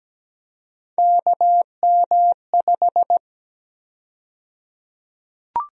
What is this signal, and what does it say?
Morse code, heard as a keyed tone, about 17 wpm, 700 Hz. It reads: KM5
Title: USA Call Sign Prefixes - Repeated 2x - 17wpm